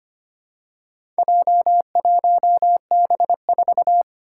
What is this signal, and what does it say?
Morse code, heard as a keyed tone, about 25 wpm, 700 Hz. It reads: J1B4
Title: Four Challenging Letter-Number Combinations 25wpm